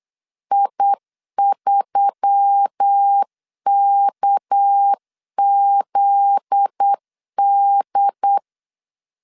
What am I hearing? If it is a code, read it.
Morse code, heard as a keyed tone, about 8 wpm, 800 Hz. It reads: I3KZD
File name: LEZIONI / 8 YJ45 / 8 Y 8wpm 3245